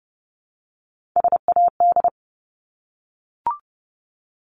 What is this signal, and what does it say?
Morse code, heard as a keyed tone, about 30 wpm, 700 Hz. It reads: SAD
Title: Top 1501-1750 Words 30wpm